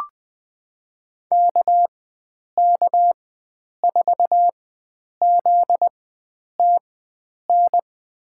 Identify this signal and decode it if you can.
Morse code, heard as a keyed tone, about 20 wpm, spaced as for 9 wpm, 700 Hz. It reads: KK4ZTN
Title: USA Call Signs - 4x Character Spacing - 20wpm